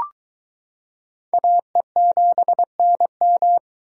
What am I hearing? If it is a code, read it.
Morse code, heard as a keyed tone, about 22 wpm, 700 Hz. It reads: AE7NM